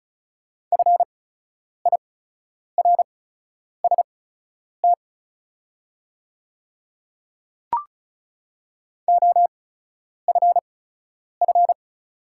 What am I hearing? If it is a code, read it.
Morse code, heard as a keyed tone, about 36 wpm, spaced as for 10 wpm, 700 Hz. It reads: FIRST OFF
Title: 8x Character Spacing - Top 200 Words - 35wpm